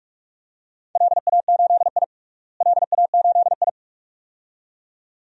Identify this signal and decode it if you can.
Morse code, heard as a keyed tone, about 46 wpm, 700 Hz. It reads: LA8I LA8I
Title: DXCC-100 Call Signs - 4 Characters - Repeated 2x - 45wpm